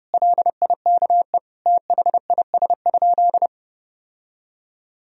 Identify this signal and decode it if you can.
Morse code, heard as a keyed tone, about 30 wpm, 700 Hz. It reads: LIKE THIS?
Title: Sentences from Top 1000 Words - Review Part 2 - 30wpm